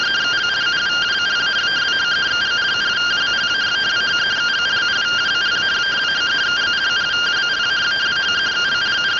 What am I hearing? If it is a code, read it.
Two-tone teleprinter signal, about 170 Hz shift.